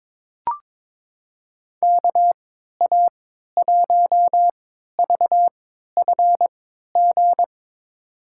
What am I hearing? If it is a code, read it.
Morse code, heard as a keyed tone, about 22 wpm, spaced as for 13 wpm, 700 Hz. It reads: KA1VFG
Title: USA Call Signs - 3x Character Spacing - 22wpm